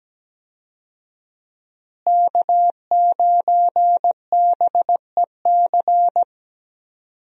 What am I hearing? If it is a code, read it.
Morse code, heard as a keyed tone, about 17 wpm, 700 Hz. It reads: K9BEC
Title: USA Call Signs - 5 Characters - Repeated 3x - 17wpm